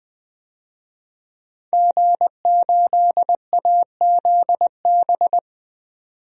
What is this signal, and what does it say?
Morse code, heard as a keyed tone, about 20 wpm, 700 Hz. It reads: G8AZB
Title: DXCC-100 Call Signs - 5 Characters - 20wpm